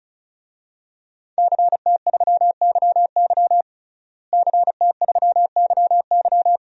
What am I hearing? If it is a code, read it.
Morse code, heard as a keyed tone, about 36 wpm, 700 Hz. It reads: CT3YY CT3YY
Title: DXCC-100 Call Signs - 5 Characters - Repeated 3x - 35wpm